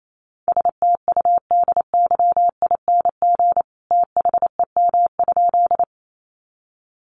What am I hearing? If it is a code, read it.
Morse code, heard as a keyed tone, about 28 wpm, 700 Hz. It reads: STUDYING THEM?
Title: A Sound Of Thunder 28wpm